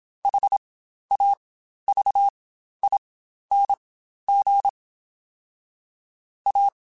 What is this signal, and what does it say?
Morse code, heard as a keyed tone, about 27 wpm, spaced as for 12 wpm, 800 Hz. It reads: HAVING A
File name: Quotes Spaced / Quotes Spaced-mp3s / ChildAsleep sp @25